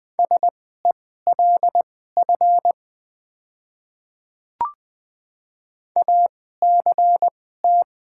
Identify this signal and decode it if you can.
Morse code, heard as a keyed tone, about 20 wpm, spaced as for 14 wpm, 700 Hz. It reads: SELF ACT
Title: 2x Character Spacing - Top 200 Words - 20wpm